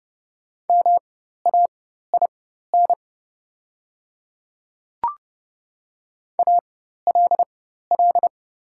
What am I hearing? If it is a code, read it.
Morse code, heard as a keyed tone, about 30 wpm, spaced as for 14 wpm, 700 Hz. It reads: MAIN ALL
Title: 4x Character Spacing - Top 500 Words - 30wpm